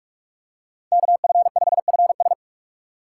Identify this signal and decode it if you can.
Morse code, heard as a keyed tone, about 43 wpm, 700 Hz. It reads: KF5FS
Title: USA Call Signs - 5 Characters - 4x Character Spacing - 45wpm